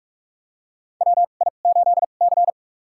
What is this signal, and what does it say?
Morse code, heard as a keyed tone, about 47 wpm, spaced as for 33 wpm, 700 Hz. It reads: WI8C
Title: USA Call Signs - 4 Characters - 2x Character Spacing - 45wpm